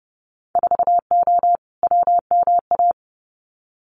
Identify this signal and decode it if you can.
Morse code, heard as a keyed tone, about 31 wpm, 700 Hz. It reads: 4O WMA
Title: Winter Field Day Exchanges Encore 30wpm